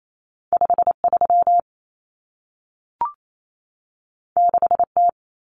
Morse code, heard as a keyed tone, about 28 wpm, 700 Hz. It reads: 53 6T
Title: Common Cut Numbers - 2-Digit 28wpm